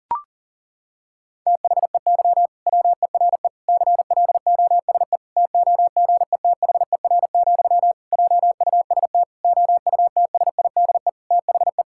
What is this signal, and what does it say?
Morse code, heard as a keyed tone, about 40 wpm, 700 Hz. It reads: THEY WERE CLOSE TOGETHER, JUST OUTSIDE THE
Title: Sentences from Top 1000 Words - Review Part 6 - 40wpm